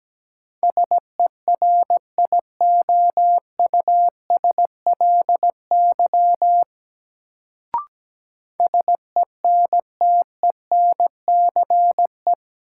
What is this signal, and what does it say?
Morse code, heard as a keyed tone, about 17 wpm, 700 Hz. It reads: SERIOUSLY SENTENCE
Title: Sentences from Top 1200 Words - New Word Drill Alt - 17wpm